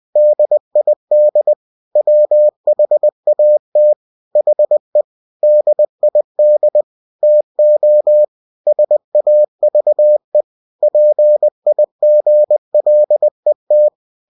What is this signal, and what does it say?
Morse code, heard as a keyed tone, about 20 wpm, 600 Hz. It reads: DID WHAT HE DID TO SAVE PIGLET
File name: cw-books / winnie-the-pooh / winnie-20wpm-page-0102